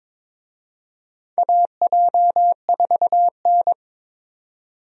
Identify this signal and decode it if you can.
Morse code, heard as a keyed tone, about 22 wpm, 700 Hz. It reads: AJ4N
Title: USA Call Signs - 4 Characters - 22wpm